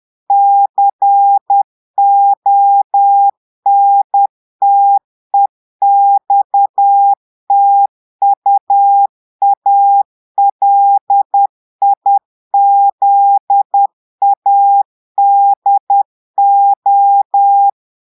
Morse code, heard as a keyed tone, about 10 wpm, 800 Hz. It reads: CONTEXTUALIZADO
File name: Portugais - texte en clair 1 - vitesse - 10 mots minute